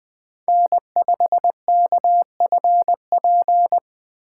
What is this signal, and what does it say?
Morse code, heard as a keyed tone, about 20 wpm, 700 Hz. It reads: N5KFP
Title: USA Call Signs - 5 Characters - 4x Character Spacing - 20wpm